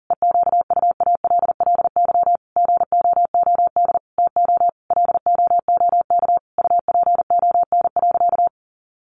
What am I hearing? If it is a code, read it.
Morse code, heard as a keyed tone, about 37 wpm, 700 Hz. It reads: EQUALLY GOOD TO LOOK UPON.
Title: Sentences from Top 1400 Words 40wpm